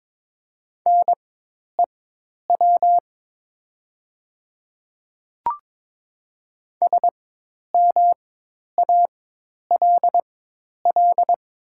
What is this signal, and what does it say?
Morse code, heard as a keyed tone, about 22 wpm, spaced as for 10 wpm, 700 Hz. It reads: NEW SMALL
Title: 4x Character Spacing - Top 300 Words - 22wpm